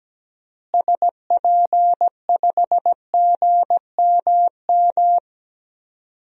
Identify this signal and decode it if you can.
Morse code, heard as a keyed tone, about 17 wpm, 700 Hz. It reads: SP5GMM
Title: DXCC-100 Call Signs - 6 Characters - Repeated 2x - 17wpm